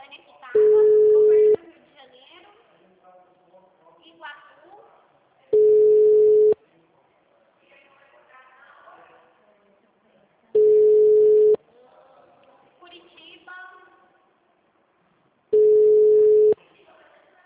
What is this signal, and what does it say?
Ringback tone (Europe, 425 Hz), with speech.
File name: out- -2014- - - .37